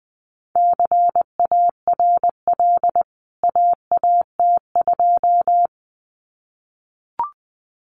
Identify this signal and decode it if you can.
Morse code, heard as a keyed tone, about 20 wpm, 700 Hz. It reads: CARL AAT2
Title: CWT Exchange 20wpm